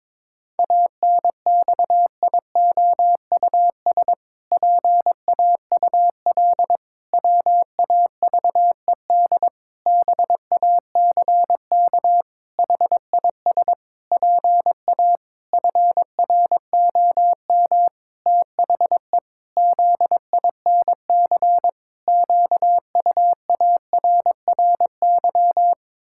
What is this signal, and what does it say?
Morse code, heard as a keyed tone, about 22 wpm, 700 Hz. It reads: ANXIOUS PAUL WAVED BACK HIS PA FROM THE ZINC QUARRY
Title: Pangram Challenge v2 - 22wpm